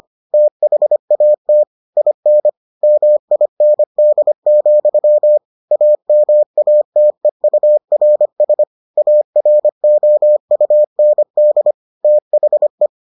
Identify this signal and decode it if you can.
Morse code, heard as a keyed tone, about 25 wpm, 600 Hz. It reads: THAT IN MIND, AMATEURS AROUND THE